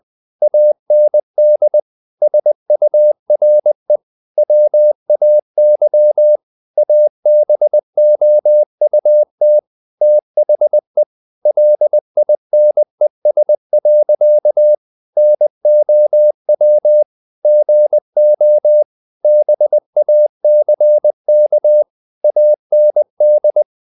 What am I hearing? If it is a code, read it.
Morse code, heard as a keyed tone, about 20 wpm, 600 Hz. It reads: AND SURE WAY ABOUT THE LINES. NOW GO BACK AND
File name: the old man and the sea part 2 20wpm